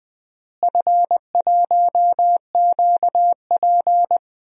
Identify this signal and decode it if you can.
Morse code, heard as a keyed tone, about 20 wpm, 700 Hz. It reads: F1QP
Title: DXCC-100 Call Signs - 4 Characters - Repeated 2x - 20wpm